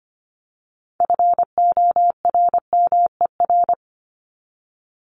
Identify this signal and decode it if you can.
Morse code, heard as a keyed tone, about 25 wpm, 700 Hz. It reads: FORMER
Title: LY Suffix - Top 100 Words with Base Words 25wpm